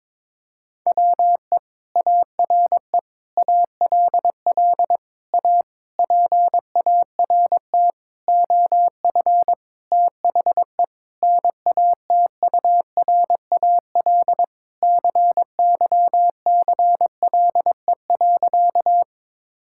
Morse code, heard as a keyed tone, about 22 wpm, 700 Hz. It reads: WE ARE ALL A PART OF THE NATURAL CYCLE.
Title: Sentences from Top 1000 Words - Review Part 6 - 22wpm